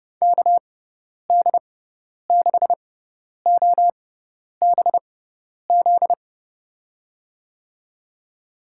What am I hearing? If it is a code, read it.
Morse code, heard as a keyed tone, about 30 wpm, spaced as for 10 wpm, 700 Hz. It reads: KD6OBZ
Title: USA Call Signs - 6x Character Spacing - 30wpm